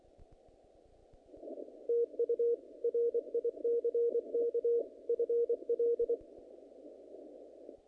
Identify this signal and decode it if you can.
Morse code, heard as a keyed tone, about 26 wpm, 470 Hz. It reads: TU RICK FL
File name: contests / naqp-cw-k5tr-op-k5pi- -2100